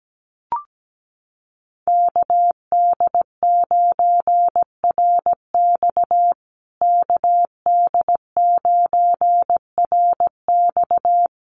Morse code, heard as a keyed tone, about 17 wpm, 700 Hz. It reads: KD9RX KD9RX
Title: USA Call Signs Repeated 3x 17wpm